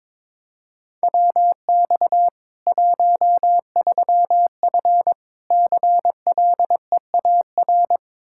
Morse code, heard as a keyed tone, about 22 wpm, 700 Hz. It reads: WX 13F CLEAR
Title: K1USN SST QSO Practice 22wpm